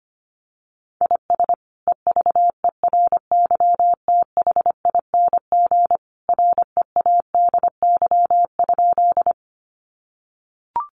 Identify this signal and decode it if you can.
Morse code, heard as a keyed tone, about 25 wpm, 700 Hz. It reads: IS EVERYTHING READY?
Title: Sentences from Top 700 Words 25wpm